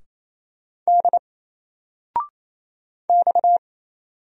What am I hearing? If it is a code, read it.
Morse code, heard as a keyed tone, about 28 wpm, 700 Hz. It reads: D X
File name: Single-Letter-Number-28wpm